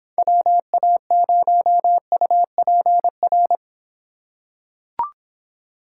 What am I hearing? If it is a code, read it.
Morse code, heard as a keyed tone, about 26 wpm, 700 Hz. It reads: WA0UPR